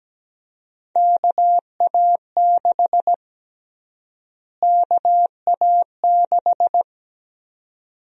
Morse code, heard as a keyed tone, about 17 wpm, 700 Hz. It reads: KA6 KA6
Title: USA Call Sign Prefixes - Repeated 2x - 17wpm